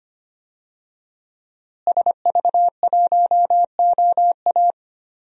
Morse code, heard as a keyed tone, about 25 wpm, 700 Hz. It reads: SV1OA
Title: DXCC-100 Call Signs - 5 Characters - Repeated 3x - 25wpm